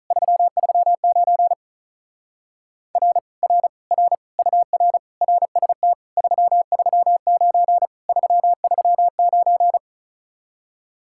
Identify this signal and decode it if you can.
Morse code, heard as a keyed tone, about 42 wpm, 700 Hz. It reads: 339 R R R UR RST 339 339
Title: Speed-Racing Common QSO Phrases Encore - 35wpm